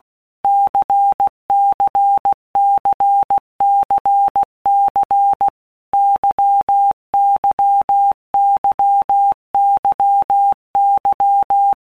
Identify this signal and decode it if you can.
Morse code, heard as a keyed tone, about 16 wpm, 800 Hz. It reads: CCCCC YYYYY
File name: Basic Letter Groups By Blocks / Groups@15 / LFCYQ@15